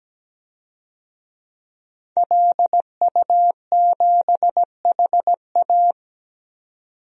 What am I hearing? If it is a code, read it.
Morse code, heard as a keyed tone, about 17 wpm, 700 Hz. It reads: LU7HA
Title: DXCC-100 Call Signs - 5 Characters - 17wpm